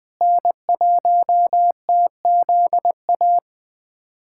Morse code, heard as a keyed tone, about 20 wpm, 700 Hz. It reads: N1TZA
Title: USA Callsign DE Callsign - 5 Characters - 20wpm